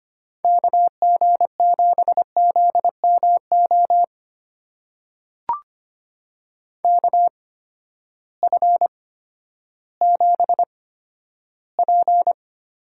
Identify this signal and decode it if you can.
Morse code, heard as a keyed tone, about 25 wpm, spaced as for 12 wpm, 700 Hz. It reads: KG7ZMO KF7P
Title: USA Call Signs - 6 Characters - 8x Character Spacing - 25wpm